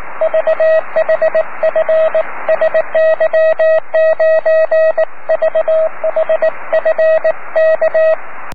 Morse code, heard as a keyed tone, about 18 wpm, 640 Hz, with radio noise. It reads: VHF SY9VHF K
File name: om3pv 70mhz / zaznam / sy9vhf